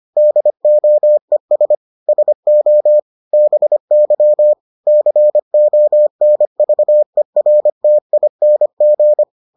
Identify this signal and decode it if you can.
Morse code, heard as a keyed tone, about 25 wpm, 600 Hz. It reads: DOES SO BY CONVERTING